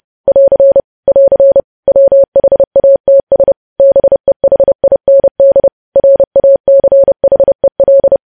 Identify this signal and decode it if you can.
Morse code, heard as a keyed tone, about 30 wpm, 550 Hz. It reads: <AR> <AR> WHATS BEHIND RACHEL
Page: News Headlines in Morse Code at 30 WPM